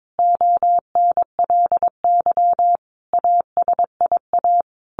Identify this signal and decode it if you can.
Morse code, heard as a keyed tone, about 22 wpm, 700 Hz. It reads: ONLY ASIA
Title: A Sound Of Thunder 22wpm